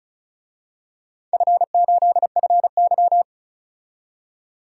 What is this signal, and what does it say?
Morse code, heard as a keyed tone, about 34 wpm, 700 Hz. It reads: F8FY
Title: DXCC-100 Call Signs - 4 Characters - 35wpm